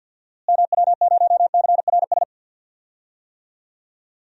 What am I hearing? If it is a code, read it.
Morse code, heard as a keyed tone, about 46 wpm, 700 Hz. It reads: MW0CRS
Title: DXCC-100 Call Signs - 6 Characters - Repeated 2x - 50wpm